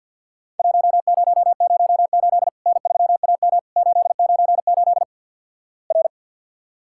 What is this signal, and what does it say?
Morse code, heard as a keyed tone, about 50 wpm, 700 Hz. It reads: 1008 N3AM 898 R
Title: Worked All Europe DX Contest - QTC Exchange with Repeats - 50wpm